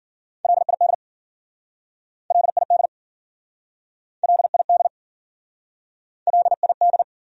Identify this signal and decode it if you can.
Morse code, heard as a keyed tone, about 53 wpm, 700 Hz. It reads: LID LID LID LID